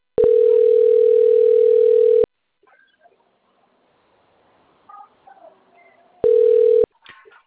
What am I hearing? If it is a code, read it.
Ringback tone (North America), 440 + 480 Hz.